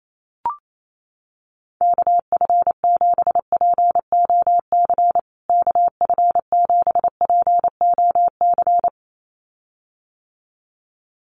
Morse code, heard as a keyed tone, about 28 wpm, 700 Hz. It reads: KF7POC KF7POC
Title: USA Call Signs Repeated 2x 28wpm